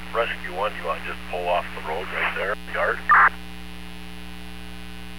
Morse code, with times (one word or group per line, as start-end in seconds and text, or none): none